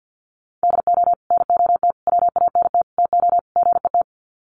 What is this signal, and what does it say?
Morse code, heard as a keyed tone, about 50 wpm, 700 Hz. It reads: DO NOT WANT TO GET
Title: Sets of 5 Words 50wpm